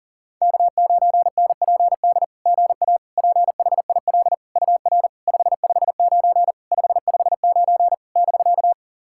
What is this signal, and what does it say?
Morse code, heard as a keyed tone, about 40 wpm, 690 Hz. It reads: K9NPD GA PHIL UR 559 559 <BK>
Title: SOTA QSO Practice 40wpm